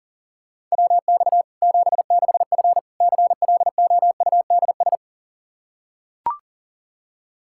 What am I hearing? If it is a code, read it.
Morse code, heard as a keyed tone, about 40 wpm, 700 Hz. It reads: WX 76F CLOUDS
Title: Practice QSO - Exchange 2 - Info - 40wpm